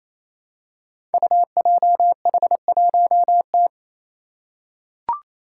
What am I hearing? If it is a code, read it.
Morse code, heard as a keyed tone, about 28 wpm, 700 Hz. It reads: UJH1T
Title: Five Challenging Letter-Number Combinations 28wpm